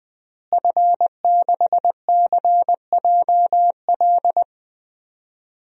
Morse code, heard as a keyed tone, about 20 wpm, 700 Hz. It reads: F6CJL
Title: DXCC-100 Call Signs - 5 Characters - Repeated 3x - 20wpm